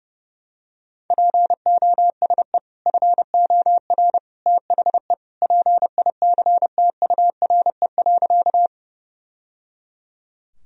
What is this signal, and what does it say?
Morse code, heard as a keyed tone, about 30 wpm, 700 Hz. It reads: POSE FOR THE PICTURE.
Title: Sentences from Top 1000 Words - Review Part 2 - 30wpm